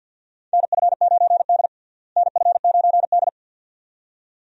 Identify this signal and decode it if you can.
Morse code, heard as a keyed tone, about 52 wpm, 700 Hz. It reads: NF9D NF9D
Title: USA Call Signs - 4 Characters - Repeated 2x - 50wpm